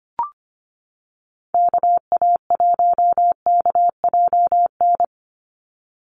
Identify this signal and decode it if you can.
Morse code, heard as a keyed tone, about 25 wpm, 700 Hz. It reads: KA1KJN
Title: USA Call Signs 25wpm